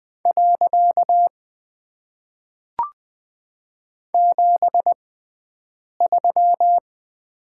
Morse code, heard as a keyed tone, about 20 wpm, spaced as for 7 wpm, 700 Hz. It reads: . 73
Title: Top 90 QSO Elements - 6x Character Spacing - 20wpm